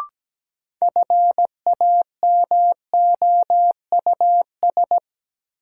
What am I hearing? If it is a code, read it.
Morse code, heard as a keyed tone, about 17 wpm, 700 Hz. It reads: FAMOUS
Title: Sentences from Top 1500 Words - New Word Drill Alt - 17wpm